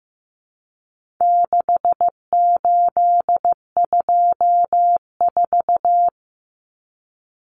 Morse code, heard as a keyed tone, about 15 wpm, 700 Hz. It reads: 6824
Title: Common Cut Numbers - 4-Digit 15wpm